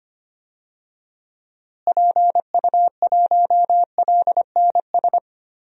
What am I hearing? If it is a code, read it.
Morse code, heard as a keyed tone, about 25 wpm, 700 Hz. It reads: PU1LNS
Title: DXCC-100 Call Signs - 6 Characters - Repeated 3x - 25wpm